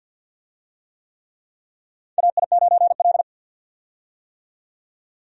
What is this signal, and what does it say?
Morse code, heard as a keyed tone, about 48 wpm, 700 Hz. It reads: AI9L
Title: USA Call Signs - 4 Characters - 50wpm